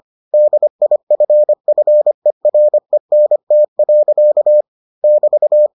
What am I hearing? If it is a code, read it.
Morse code, heard as a keyed tone, about 25 wpm, 600 Hz. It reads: DIFFERENT. <BT>